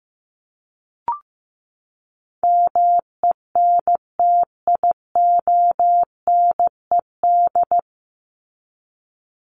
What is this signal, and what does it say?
Morse code, heard as a keyed tone, about 15 wpm, 700 Hz. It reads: MENTIONED
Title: ED Suffix - Top 100 Words with Base Words 15wpm